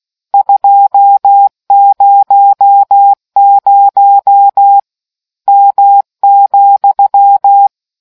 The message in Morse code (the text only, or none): 200 M,